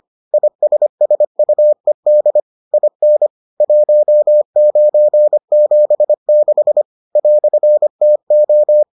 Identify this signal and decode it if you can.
Morse code, heard as a keyed tone, about 25 wpm, 600 Hz. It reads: ISSUED IN 1976 "TO